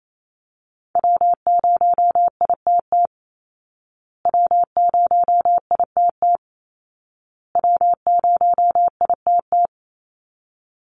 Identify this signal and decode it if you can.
Morse code, heard as a keyed tone, about 28 wpm, 700 Hz. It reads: W0ITT W0ITT W0ITT
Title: Call Signs Repeated 2x - 28wpm